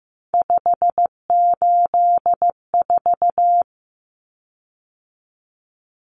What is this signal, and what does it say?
Morse code, heard as a keyed tone, about 15 wpm, 700 Hz. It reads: 584
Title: Common Cut Numbers - 3-Digit 15wpm